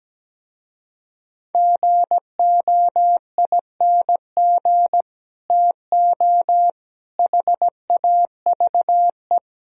Morse code, heard as a keyed tone, about 17 wpm, 700 Hz. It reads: GOING TO HAVE
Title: Speed-Racing - Sets of 4 Words 17wpm